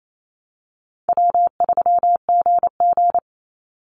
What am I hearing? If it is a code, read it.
Morse code, heard as a keyed tone, about 28 wpm, 700 Hz. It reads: W3GG
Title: Call Signs Repeated 3x - 28wpm